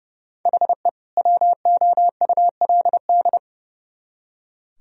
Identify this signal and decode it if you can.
Morse code, heard as a keyed tone, about 29 wpm, 700 Hz. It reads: HE WOULD
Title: Speed-Racing - Sets of 2 Words 30wpm